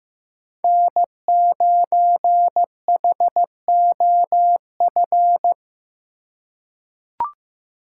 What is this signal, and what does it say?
Morse code, heard as a keyed tone, about 15 wpm, 700 Hz. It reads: N9HOF
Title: USA Call Signs - 5 Characters - 4x Character Spacing - 15wpm